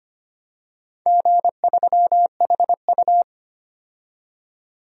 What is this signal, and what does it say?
Morse code, heard as a keyed tone, about 25 wpm, 700 Hz. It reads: G3HU
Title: DXCC-100 Call Signs - 4 Characters - 25wpm